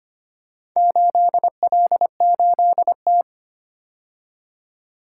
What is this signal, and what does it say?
Morse code, heard as a keyed tone, about 25 wpm, 700 Hz. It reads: 8L8T
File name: practice / 25wpm / Four Letter-Number Combinations - 25wpm